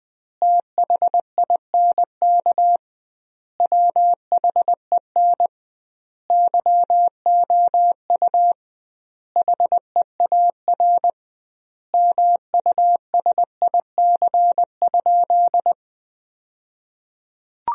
Morse code, heard as a keyed tone, about 20 wpm, 700 Hz. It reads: THINK WHEN YOU HEAR MUSIC?
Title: 2x Word-Spacing - Sentences from Top 300 Words - 20wpm